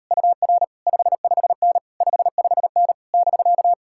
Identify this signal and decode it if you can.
Morse code, heard as a keyed tone, about 38 wpm, 700 Hz. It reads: UR 55N 55N <BK>